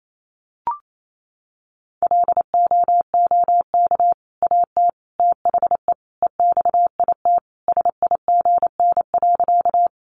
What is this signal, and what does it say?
Morse code, heard as a keyed tone, about 28 wpm, 700 Hz. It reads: LOOK AT THE EXIT SIGN.
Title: Sentences from Top 1100 Words 28wpm